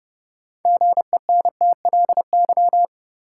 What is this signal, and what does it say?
Morse code, heard as a keyed tone, about 32 wpm, 700 Hz. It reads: GENTLY
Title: Sentences from Top 1500 Words - New Word Drill Alt - 30wpm